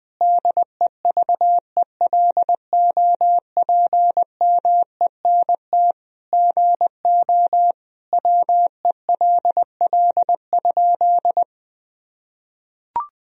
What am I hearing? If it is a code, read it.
Morse code, heard as a keyed tone, about 20 wpm, 700 Hz. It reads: DEVELOPMENT GO WELL?
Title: Sentences from Top 1000 Words - Review Part 4 - 20wpm